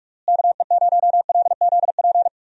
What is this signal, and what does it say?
Morse code, heard as a keyed tone, about 41 wpm, 700 Hz. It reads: KE0LZP
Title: USA Call Signs - 6 Characters - Repeated 3x - 45wpm